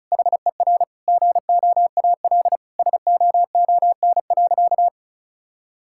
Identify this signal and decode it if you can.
Morse code, heard as a keyed tone, about 36 wpm, 700 Hz. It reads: HER GOAL SOON.
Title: Sentences from Top 1000 Words - Review Part 6 - 35wpm